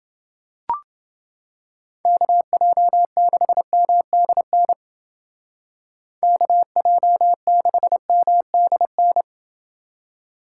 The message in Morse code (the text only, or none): KJ6MDN KJ6MDN